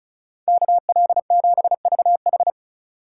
Morse code, heard as a keyed tone, about 43 wpm, 700 Hz. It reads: KL7VH
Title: Speed Racing - USA 2x2 Call Signs 35wpm